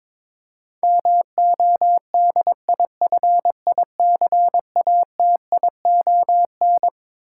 Morse code, heard as a keyed tone, about 22 wpm, 700 Hz. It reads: MODIFICATION
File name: Twelve Letter Words - 22wpm